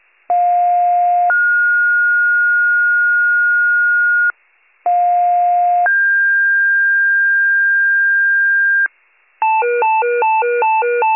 Two-tone page, heard as a one-second tone, then a three-second tone, about 710 then 1430 Hz, about 710 then 1600 Hz, with noise. It is heard from 0.3 to 8.9 s.